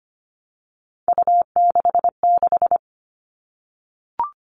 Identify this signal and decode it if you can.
Morse code, heard as a keyed tone, about 25 wpm, 700 Hz. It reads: U66
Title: All Cut Numbers - 3-Digit 25wpm